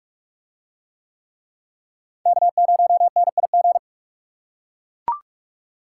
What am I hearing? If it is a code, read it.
Morse code, heard as a keyed tone, about 41 wpm, 700 Hz. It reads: K0NIG